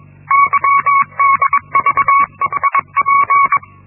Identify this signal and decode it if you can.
Morse code, heard as a keyed tone, about 21 wpm, 1120 Hz. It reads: YDVEIP
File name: Linksmakalnis-1